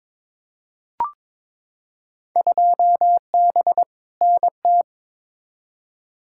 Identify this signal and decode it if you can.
Morse code, heard as a keyed tone, about 22 wpm, 700 Hz. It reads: 2B NT
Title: Field Day Exchanges Encore - 22wpm